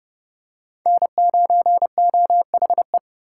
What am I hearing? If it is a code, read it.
Morse code, heard as a keyed tone, about 29 wpm, 700 Hz. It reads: N9OHE